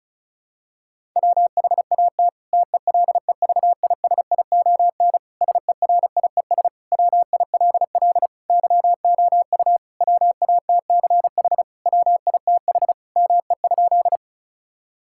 Morse code, heard as a keyed tone, about 34 wpm, 700 Hz. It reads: WHAT TELEVISION SERIES WILL YOU WATCH WITH ME?
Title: Sentences from Top 1000 Words - Review Part 5 - 35wpm